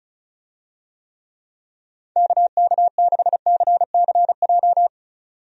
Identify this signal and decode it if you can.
Morse code, heard as a keyed tone, about 36 wpm, 700 Hz. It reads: KK6CCJ
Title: USA Call Signs - 6 Characters - 35wpm